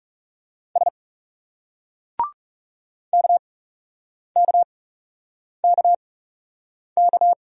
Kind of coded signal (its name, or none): Morse code